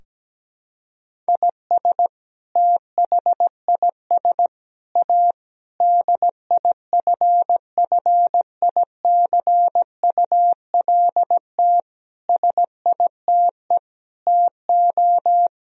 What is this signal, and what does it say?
Morse code, heard as a keyed tone, about 17 wpm, 700 Hz. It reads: IS THIS A DIFFICULT SITE TO
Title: Sentences from Top 1000 Words - Review Part 4 - 17wpm